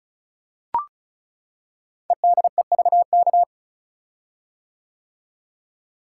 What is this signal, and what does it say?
Morse code, heard as a keyed tone, about 36 wpm, 690 Hz. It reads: EDEVK
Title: Code Groups - 5 Letters - 35wpm